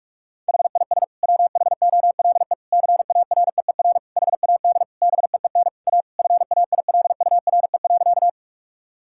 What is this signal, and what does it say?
Morse code, heard as a keyed tone, about 47 wpm, 700 Hz. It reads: HIS WHOLE CAREER HAD BEEN A FAILURE.
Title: Sentences from Top 1000 Words - Review Part 6 - 45wpm